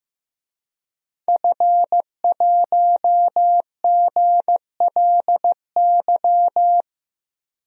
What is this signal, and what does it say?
Morse code, heard as a keyed tone, about 15 wpm, 700 Hz. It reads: F1GLY
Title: DXCC-100 Call Signs - 5 Characters - Repeated 2x - 15wpm